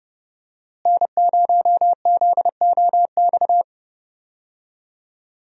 Morse code, heard as a keyed tone, about 30 wpm, 700 Hz. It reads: N0ZOX
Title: USA Call Signs - 5 Characters - Repeated 2x - 30wpm